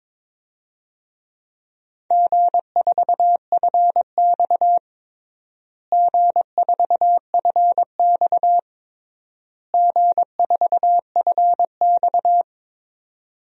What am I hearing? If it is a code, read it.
Morse code, heard as a keyed tone, about 22 wpm, 700 Hz. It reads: G4FX G4FX G4FX
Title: DXCC-100 Call Signs - 4 Characters - Repeated 3x - 22wpm